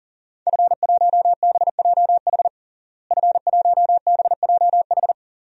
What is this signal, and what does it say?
Morse code, heard as a keyed tone, about 40 wpm, 700 Hz. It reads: F1BJH F1BJH
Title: DXCC-100 Call Signs - 5 Characters - Repeated 2x - 40wpm